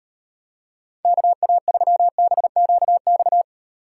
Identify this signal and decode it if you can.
Morse code, heard as a keyed tone, about 38 wpm, 700 Hz. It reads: KA3BQX